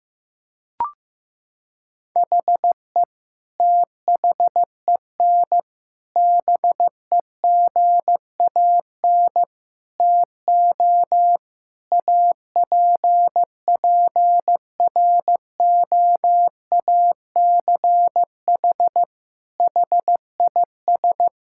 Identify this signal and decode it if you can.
Morse code, heard as a keyed tone, about 16 wpm, 700 Hz. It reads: HE THEN BEGAN TO APPROACH HIS
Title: Sentences from Top 1000 Words - Review Part 6 - 15wpm